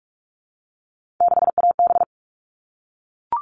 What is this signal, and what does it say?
Morse code, heard as a keyed tone, about 44 wpm, 700 Hz. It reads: 6AB
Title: All Cut Numbers - 3-Digit 45wpm